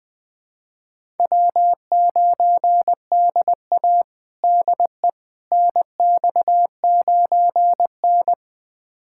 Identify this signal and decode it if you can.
Morse code, heard as a keyed tone, about 20 wpm, 700 Hz. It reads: W9DA DE NX9N
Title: USA Callsign DE Callsign - 4 Characters - 20wpm